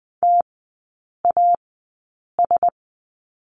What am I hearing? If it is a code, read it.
Morse code, heard as a keyed tone, about 20 wpm, spaced as for 8 wpm, 700 Hz. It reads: TAS
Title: 067 Lesson 12e - Letter d - Three Characters 20wpm